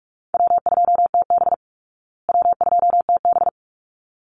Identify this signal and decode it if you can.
Morse code, heard as a keyed tone, about 44 wpm, 700 Hz. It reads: W2TB W2TB
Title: Easy Call Signs Repeated 2x - 45wpm